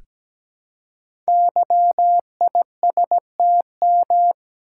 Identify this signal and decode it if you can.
Morse code, heard as a keyed tone, about 17 wpm, 700 Hz. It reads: YISTM
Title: Code Groups - 5 Letters - 17wpm